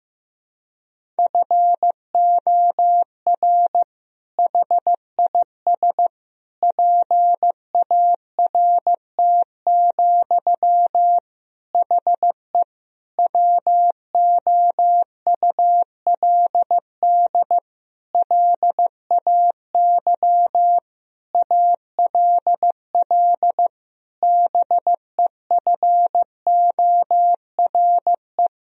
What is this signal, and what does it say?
Morse code, heard as a keyed tone, about 15 wpm, 700 Hz. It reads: FOR HIS PART, HE WOULD LAY ALL BEFORE
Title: Sentences from Top 1000 Words - Review Part 2 - 15wpm